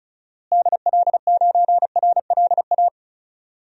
Morse code, heard as a keyed tone, about 36 wpm, 700 Hz. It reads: DL9RLA
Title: DXCC-100 Call Signs - 6 Characters - Repeated 3x - 35wpm